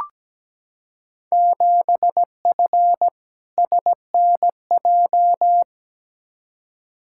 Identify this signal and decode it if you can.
Morse code, heard as a keyed tone, about 17 wpm, 700 Hz. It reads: 7F SNJ
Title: Field Day Exchanges - 17wpm